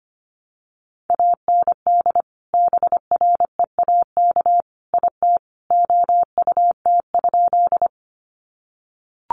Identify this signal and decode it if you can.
Morse code, heard as a keyed tone, about 25 wpm, 700 Hz. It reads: AND BREAK IT OUT?
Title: Sentences from Top 900 Words 25wpm